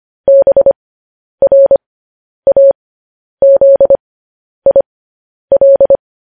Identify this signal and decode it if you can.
Morse code, heard as a keyed tone, about 25 wpm, spaced as for 10 wpm, 550 Hz. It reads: BRAZIL
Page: news-headlines-in-morse-code-at-10-wpm